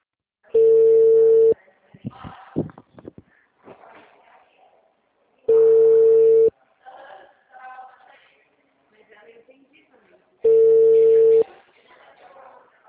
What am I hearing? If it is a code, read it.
Ringback tone (China, 450 Hz), with speech.
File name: out- -2007- - - .48191